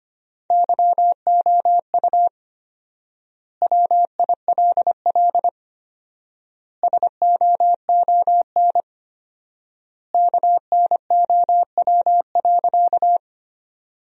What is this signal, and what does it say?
Morse code, heard as a keyed tone, about 25 wpm, 700 Hz. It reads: YOU WILL SOON KNOW.
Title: 4x Word-Spacing - Sentences from Top 400 Words - 25wpm